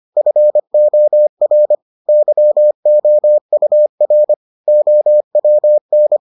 Morse code, heard as a keyed tone, about 25 wpm, 600 Hz. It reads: FOR YOUR OWN